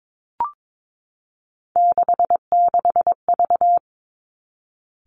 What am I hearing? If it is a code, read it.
Morse code, heard as a keyed tone, about 22 wpm, 700 Hz. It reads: 66V
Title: Three Challenging Letter-Number Combinations 22wpm